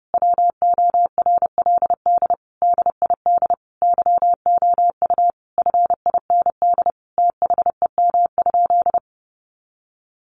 Morse code, heard as a keyed tone, about 30 wpm, 700 Hz. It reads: WORLD DID YOU FIND THEM?
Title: Sentences from Top 200 Words 30wpm